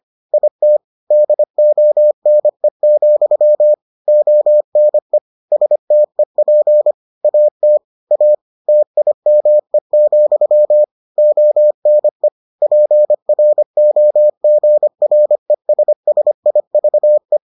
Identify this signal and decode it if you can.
Morse code, heard as a keyed tone, about 25 wpm, 600 Hz. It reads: IT DONE, ONE STEP AT A TIME, ONE PROGRESSIVE